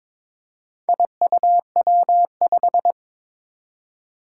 Morse code, heard as a keyed tone, about 22 wpm, 700 Hz. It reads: IUW5
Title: Four Challenging Letter-Number Combinations 22wpm